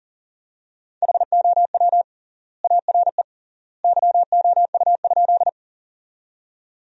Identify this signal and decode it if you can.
Morse code, heard as a keyed tone, about 40 wpm, 700 Hz. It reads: HOW ARE YOU?
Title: 3x Word-Spacing - Sentences from Top 100 Words - 40wpm